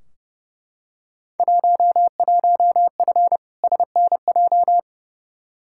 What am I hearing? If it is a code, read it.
Morse code, heard as a keyed tone, about 30 wpm, 700 Hz. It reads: 11F SNJ
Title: Field Day Exchanges Encore - 30wpm